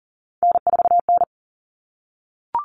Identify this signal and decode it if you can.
Morse code, heard as a keyed tone, about 40 wpm, 700 Hz. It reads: N4N
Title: Common Cut Numbers - 3-Digit 40wpm